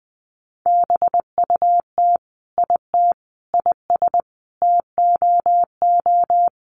Morse code, heard as a keyed tone, about 20 wpm, 700 Hz. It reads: BUT IT IS TOO
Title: Sentences from Top 1100 Words 20wpm